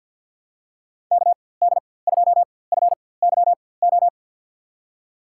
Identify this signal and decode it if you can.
Morse code, heard as a keyed tone, about 48 wpm, spaced as for 23 wpm, 700 Hz. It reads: KD2FYO
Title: USA Call Signs - 6 Characters - 4x Character Spacing - 50wpm